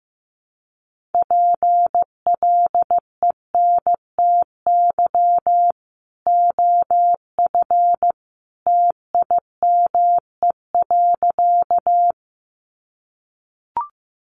Morse code, heard as a keyed tone, about 15 wpm, 700 Hz. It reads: PLENTY OF TIME.
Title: Sentences from Top 1300 Words 15wpm